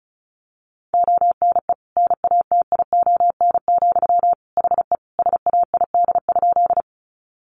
Morse code, heard as a keyed tone, about 34 wpm, 700 Hz. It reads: ONE NATION, HE SAID?
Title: Sentences from Top 900 Words 35wpm